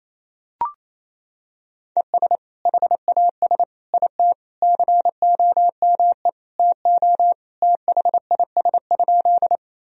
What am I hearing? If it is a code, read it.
Morse code, heard as a keyed tone, about 28 wpm, 700 Hz. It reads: ES HAS IT COME TO THIS?
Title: Sentences from Top 100 Words and Abbreviations 28wpm